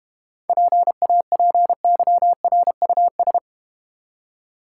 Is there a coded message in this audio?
Yes: Morse code from 0.5 to 3.4 s, heard as a keyed tone, about 33 wpm, 700 Hz. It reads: PAPYRUS